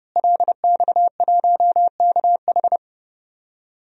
Morse code, heard as a keyed tone, about 32 wpm, 700 Hz. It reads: LX1KH
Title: DXCC-100 Call Signs - 5 Characters - Repeated 3x - 30wpm